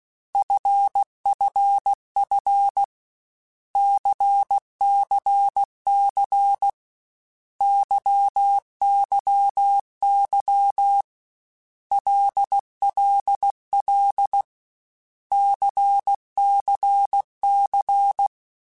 Morse code, heard as a keyed tone, about 16 wpm, 800 Hz. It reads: FFF CCC YYY LLL CCC
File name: Basic Letter Groups By7Groups / LFCYQ / LFCYQ@13